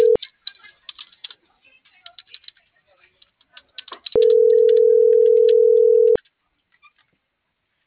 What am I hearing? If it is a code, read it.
Ringback tone (North America), 440 + 480 Hz.